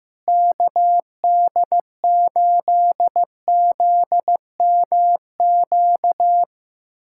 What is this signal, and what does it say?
Morse code, heard as a keyed tone, about 16 wpm, 700 Hz. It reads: KD8ZMQ
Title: USA Callsign DE Callsign 15wpm